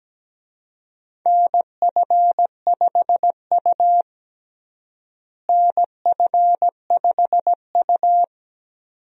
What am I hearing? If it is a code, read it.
Morse code, heard as a keyed tone, about 17 wpm, 700 Hz. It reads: NF5U NF5U
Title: USA Call Signs - 4 Characters - Repeated 2x - 17wpm